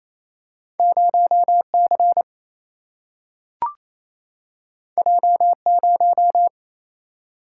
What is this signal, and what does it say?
Morse code, heard as a keyed tone, about 28 wpm, 700 Hz. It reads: 0 C J 0
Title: Single-Letter Single-Number 28wpm